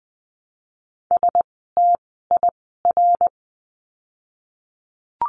Morse code, heard as a keyed tone, about 20 wpm, 700 Hz. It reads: S T I R
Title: 068 Lesson 12f - Letter d - Words 20wpm